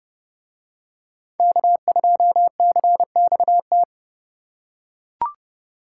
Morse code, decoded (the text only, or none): K2CXT